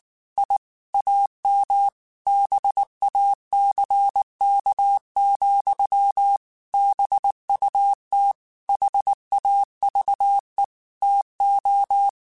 Morse code, heard as a keyed tone, about 19 wpm, 800 Hz. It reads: I AM BACK, BUT HAVE TO
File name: QSOs / QSOs-mp3s / kc7haa@18